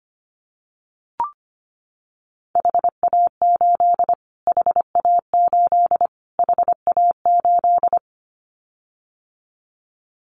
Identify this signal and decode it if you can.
Morse code, heard as a keyed tone, about 25 wpm, 700 Hz. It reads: HA8 HA8 HA8
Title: Callsign Prefixes for Top 25 DXCC - Repeated 3x - 25wpm